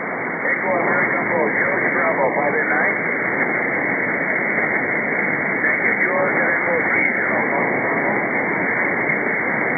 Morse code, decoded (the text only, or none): none